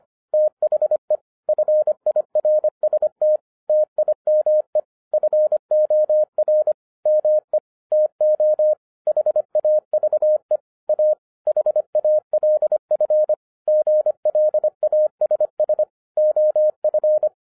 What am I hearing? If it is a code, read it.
Morse code, heard as a keyed tone, about 26 wpm, 610 Hz. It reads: THE FIRST TIME FOR ME TO HAVE A HALF GLASS OF